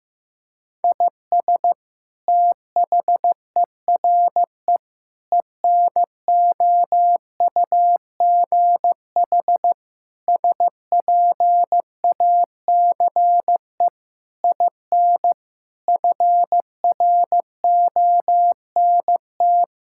Morse code, heard as a keyed tone, about 15 wpm, 700 Hz. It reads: IS THERE ENOUGH SPACE IN FRONT
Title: Sentences from Top 1000 Words - Review Part 5 - 15wpm